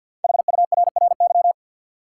Morse code, heard as a keyed tone, about 49 wpm, 700 Hz. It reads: HURRY
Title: Five Letter Words - 50wpm